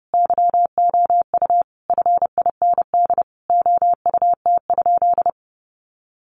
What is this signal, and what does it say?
Morse code, heard as a keyed tone, about 30 wpm, 700 Hz. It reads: YOU FIND OUT?
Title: Sentences from Top 200 Words 30wpm